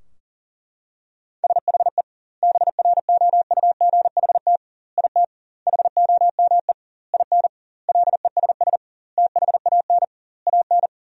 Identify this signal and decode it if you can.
Morse code, heard as a keyed tone, about 40 wpm, 700 Hz. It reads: SHE BROUGHT IT HOME IN LESS THAN AN
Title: 2x Word-Spacing - Sentences from Top 500 Words - 40wpm